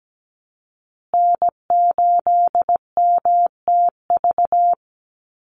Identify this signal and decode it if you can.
Morse code, heard as a keyed tone, about 17 wpm, 700 Hz. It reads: N8MTV
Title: Call Signs Repeated 3x - 17wpm